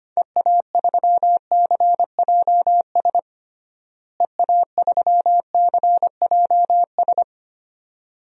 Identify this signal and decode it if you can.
Morse code, heard as a keyed tone, about 25 wpm, 700 Hz. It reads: EA3CJS EA3CJS
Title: DXCC-100 Call Signs - 6 Characters - Repeated 2x - 25wpm